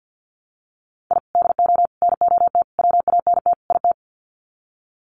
Morse code, heard as a keyed tone, about 50 wpm, 700 Hz. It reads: I DO NOT WANT IT
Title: Sets of 5 Words Encore2 50wpm